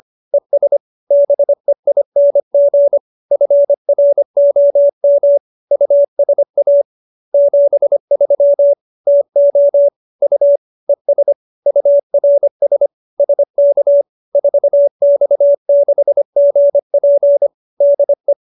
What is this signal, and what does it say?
Morse code, heard as a keyed tone, about 25 wpm, 570 Hz. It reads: ES BEING FROM USA 73 TO U ES URS SK 4X6GP DE
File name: QSO315 25